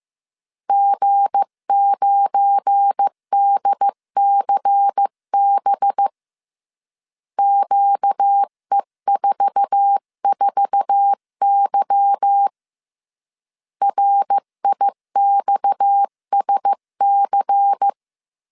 Morse code, heard as a keyed tone, about 15 wpm, 800 Hz. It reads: G9DCB QE44Y RIXSC
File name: CW / DET / 1 9 1500C S B 14wpm 3302